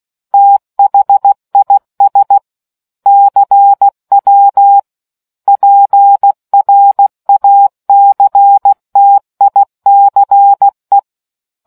Morse code, heard as a keyed tone, about 16 wpm, 800 Hz. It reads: THIS CW PRACTICE